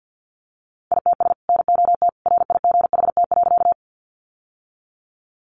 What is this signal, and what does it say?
Morse code, heard as a keyed tone, about 48 wpm, 700 Hz. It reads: ITS NOT RIGHT.
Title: Sentences from Top 600 Words 50wpm